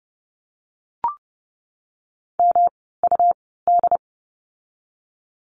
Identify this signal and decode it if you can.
Morse code, heard as a keyed tone, about 30 wpm, spaced as for 17 wpm, 700 Hz. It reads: MUD
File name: course / 30:10wpm / 108 Lesson 18f - Letter m - Words 30wpm